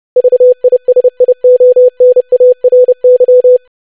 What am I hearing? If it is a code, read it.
Morse code, heard as a keyed tone, about 30 wpm, 500 Hz. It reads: VISIONARY
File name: lesson / web / longwords-25wpm